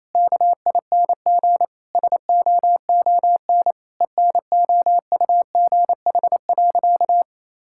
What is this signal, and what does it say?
Morse code, heard as a keyed tone, about 28 wpm, 700 Hz. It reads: KING SOON ENOUGH.
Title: Sentences from Top 1000 Words - Review Part 6 - 28wpm